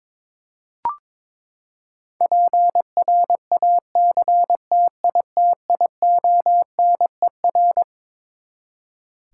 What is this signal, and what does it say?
Morse code, heard as a keyed tone, about 22 wpm, 700 Hz. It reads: PRACTITIONER
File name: Twelve Letter Words - 22wpm